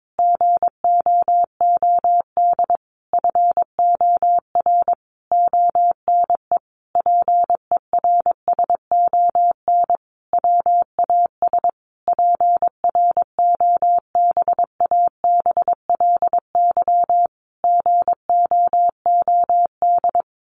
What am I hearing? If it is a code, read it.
Morse code, heard as a keyed tone, about 22 wpm, 700 Hz. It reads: GOOD FOR ONE PERSON WAS PROBABLY GOOD
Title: Sentences from Top 1400 Words 22wpm